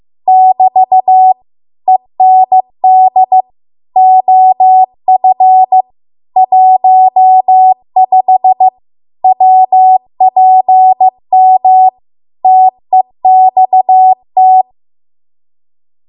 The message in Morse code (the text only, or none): <BT> END OF 15 WPM TEXT